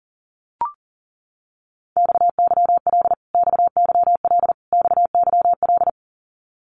Morse code, heard as a keyed tone, about 40 wpm, 700 Hz. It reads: XYL XYL XYL
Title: Abbreviations Repeated 3x - 40wpm